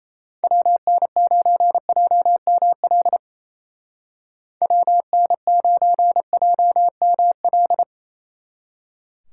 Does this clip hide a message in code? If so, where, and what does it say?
Yes: Morse code from 0.4 to 7.8 s, heard as a keyed tone, about 37 wpm, 700 Hz. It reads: WN9JML WN9JML